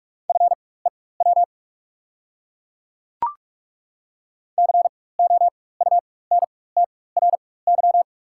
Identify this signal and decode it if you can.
Morse code, heard as a keyed tone, about 45 wpm, spaced as for 21 wpm, 690 Hz. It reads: FEW COUNTRY
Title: 4x Character Spacing - Top 300 Words - 45wpm